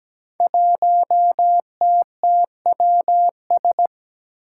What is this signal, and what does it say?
Morse code, heard as a keyed tone, about 17 wpm, 700 Hz. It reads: 1TTWS
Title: Five Challenging Letter-Number Combinations 17wpm